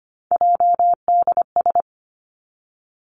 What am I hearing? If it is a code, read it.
Morse code, heard as a keyed tone, about 25 wpm, 700 Hz. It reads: JDS
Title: Three Challenging Letter-Number Combinations 25wpm